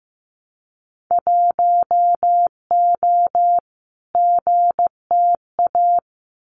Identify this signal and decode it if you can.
Morse code, heard as a keyed tone, about 15 wpm, 700 Hz. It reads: 1O GTA
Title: Winter Field Day Exchanges 15wpm